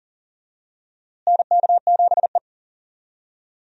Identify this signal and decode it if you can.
Morse code, heard as a keyed tone, about 40 wpm, 700 Hz. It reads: NK7E
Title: USA Call Signs - 4 Characters - Repeated 2x - 40wpm